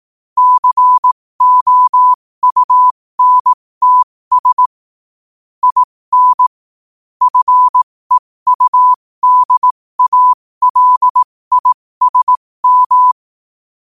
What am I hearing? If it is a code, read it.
Morse code, heard as a keyed tone, about 18 wpm, spaced as for 15 wpm, 1000 Hz. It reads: COUNTS IN FEUDALISM
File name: qotdfiles / 15wpm